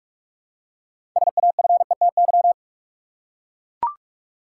Morse code, heard as a keyed tone, about 42 wpm, 700 Hz. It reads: SAFETY
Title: Six Letter Words - 45wpm